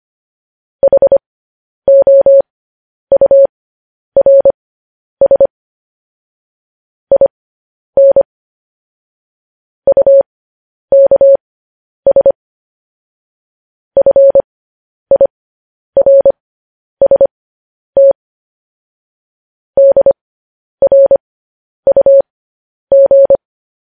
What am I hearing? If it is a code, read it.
Morse code, heard as a keyed tone, about 25 wpm, spaced as for 10 wpm, 550 Hz. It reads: HOURS IN UKS FIRST DRUG
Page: news-headlines-in-morse-code-at-10-wpm